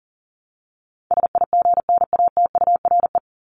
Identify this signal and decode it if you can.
Morse code, heard as a keyed tone, about 40 wpm, 700 Hz. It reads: SIGNATURE
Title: Top 1751-2000 Words 40wpm